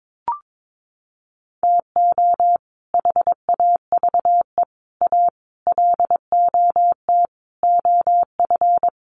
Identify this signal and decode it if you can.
Morse code, heard as a keyed tone, about 22 wpm, 700 Hz. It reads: TO HAVE A LOT OF
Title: Sets of 5 Words Encore 22wpm